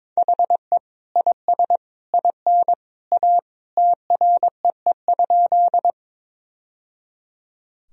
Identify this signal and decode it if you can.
Morse code, heard as a keyed tone, about 21 wpm, 700 Hz. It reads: HE IS IN A TREE?
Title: Sentences from Top 1000 Words - Review Part 1 - 22wpm